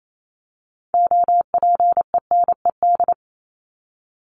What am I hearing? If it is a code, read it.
Morse code, heard as a keyed tone, about 28 wpm, 700 Hz. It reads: OPENED
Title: ED Suffix - Top 100 Words with Base Words 28wpm